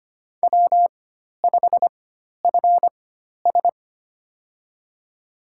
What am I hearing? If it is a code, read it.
Morse code, heard as a keyed tone, about 25 wpm, spaced as for 12 wpm, 700 Hz. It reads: W5FS
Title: USA Call Signs - 4x Character Spacing - 25wpm